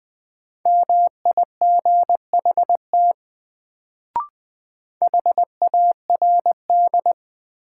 Morse code, heard as a keyed tone, about 20 wpm, 700 Hz. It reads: MIGHT HARD
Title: Sentences from Top 300 Words - New Word Drill - 20wpm